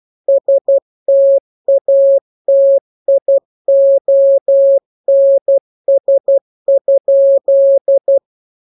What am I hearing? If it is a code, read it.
Morse code, heard as a keyed tone, about 12 wpm, 550 Hz. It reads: STATIONS?